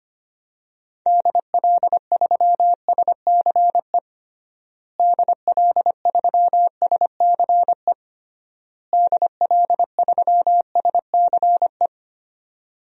Morse code, heard as a keyed tone, about 25 wpm, 700 Hz. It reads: DL3SCE DL3SCE DL3SCE
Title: DXCC-100 Call Signs - 6 Characters - Repeated 3x - 25wpm